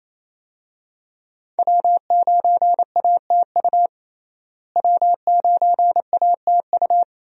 Morse code, heard as a keyed tone, about 27 wpm, 700 Hz. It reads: W9ATU W9ATU
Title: USA Call Signs - 5 Characters - Repeated 3x - 28wpm